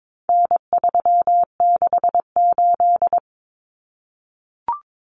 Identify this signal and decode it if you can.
Morse code, heard as a keyed tone, about 22 wpm, 700 Hz. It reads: N368
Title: Common Cut Numbers - 4-Digit 22wpm